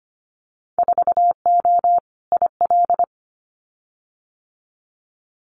Morse code, heard as a keyed tone, about 25 wpm, 700 Hz. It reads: 4O IL
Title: Winter Field Day Exchanges Encore 25wpm